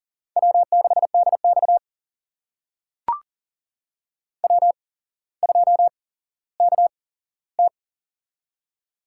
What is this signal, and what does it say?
Morse code, heard as a keyed tone, about 40 wpm, spaced as for 17 wpm, 700 Hz. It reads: W6DX W2KT
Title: USA Call Signs - 4 Characters - 8x Character Spacing - 40wpm